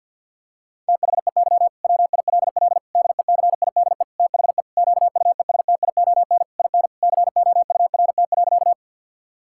Morse code, heard as a keyed tone, about 50 wpm, 700 Hz. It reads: THEY WILL DECIDE THE QUESTION IN COURT.
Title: Sentences from Top 1600 Words 50wpm